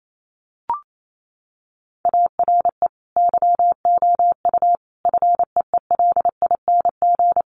Morse code, heard as a keyed tone, about 28 wpm, 700 Hz. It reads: ARE YOU FEELING
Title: Sentences from Top 1200 Words 28wpm